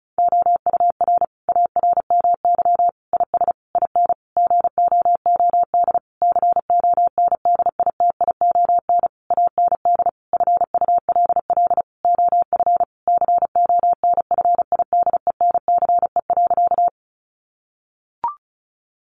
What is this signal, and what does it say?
Morse code, heard as a keyed tone, about 34 wpm, 700 Hz. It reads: OUR ARMY IS IN GOOD CONDITION AND FULL OF CONFIDENCE.
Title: Sentences from Top 1500 Words 35wpm